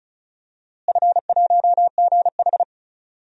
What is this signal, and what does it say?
Morse code, heard as a keyed tone, about 34 wpm, 700 Hz. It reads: F1GH
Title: DXCC-100 Call Signs - 4 Characters - Repeated 2x - 35wpm